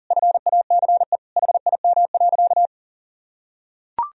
Morse code, heard as a keyed tone, about 40 wpm, 700 Hz. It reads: FACE HIM.
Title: Sentences from Top 1000 Words - Review Part 2 - 40wpm